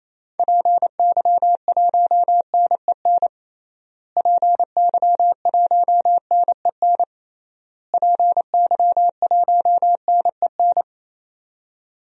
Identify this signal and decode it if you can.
Morse code, heard as a keyed tone, about 27 wpm, 700 Hz. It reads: PY1NEN PY1NEN PY1NEN
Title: DXCC-100 Call Signs - 6 Characters - Repeated 3x - 28wpm